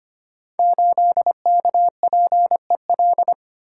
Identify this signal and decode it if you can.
Morse code, heard as a keyed tone, about 25 wpm, 700 Hz. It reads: 8KPEL
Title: Code Groups - 5 Letter:Numbers - 25wpm